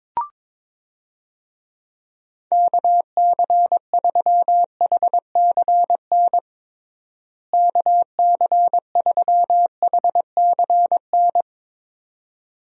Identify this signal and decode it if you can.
Morse code, heard as a keyed tone, about 22 wpm, 700 Hz. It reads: KC3HCN KC3HCN
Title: USA Call Signs - 6 Characters - Repeated 2x - 22wpm